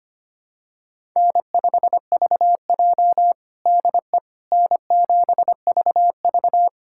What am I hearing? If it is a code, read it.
Morse code, heard as a keyed tone, about 25 wpm, 700 Hz. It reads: N5VJ DE N7VV
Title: USA Callsign DE Callsign - 4 Characters - 25wpm